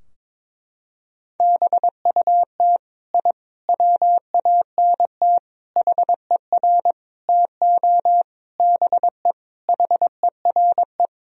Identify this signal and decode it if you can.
Morse code, heard as a keyed tone, about 22 wpm, 700 Hz. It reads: BUT I WANT HER TO BE HERE
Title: Sentences from Top 1000 Words - Review Part 4 - 22wpm